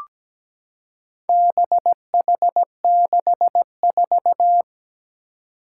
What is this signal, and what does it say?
Morse code, heard as a keyed tone, about 17 wpm, 700 Hz. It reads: BH64
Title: Four Challenging Letter-Number Combinations 17wpm